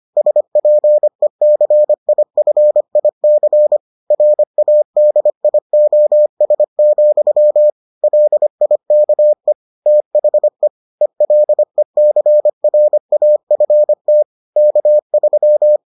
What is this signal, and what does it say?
Morse code, heard as a keyed tone, about 25 wpm, 600 Hz. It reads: SPECIFIC RADIOS, LIKE THE ELECRAFT K3